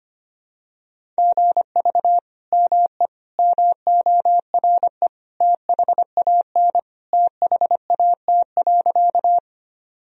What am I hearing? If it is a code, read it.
Morse code, heard as a keyed tone, about 25 wpm, 700 Hz. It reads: GV ME MORE THAN THAT.
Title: Sentences from Top 200 Words and Abbreviations 25wpm